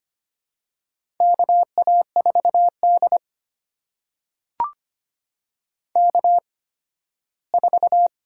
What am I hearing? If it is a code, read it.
Morse code, heard as a keyed tone, about 25 wpm, spaced as for 15 wpm, 700 Hz. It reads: KA4D K4
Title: USA Call Signs - 8x Character Spacing - 25wpm